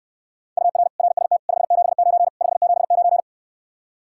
Morse code, heard as a keyed tone, about 69 wpm, 700 Hz. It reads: UR RST 579 579